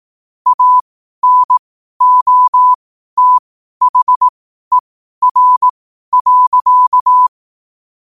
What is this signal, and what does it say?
Morse code, heard as a keyed tone, about 18 wpm, spaced as for 12 wpm, 1000 Hz. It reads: ANOTHER.
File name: qotdfiles / 12wpm